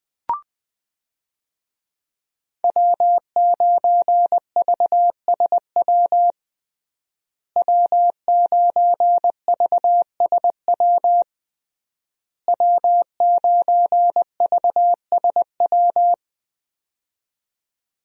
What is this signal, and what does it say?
Morse code, heard as a keyed tone, about 20 wpm, 700 Hz. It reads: W9VSW W9VSW W9VSW
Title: USA Call Signs - 5 Characters - Repeated 3x - 20wpm